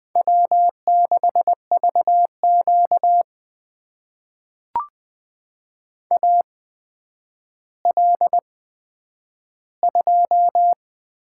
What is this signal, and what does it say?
Morse code, heard as a keyed tone, about 20 wpm, spaced as for 10 wpm, 700 Hz. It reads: W6VQ AL2
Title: USA Call Signs - 4 Characters - 8x Character Spacing - 20wpm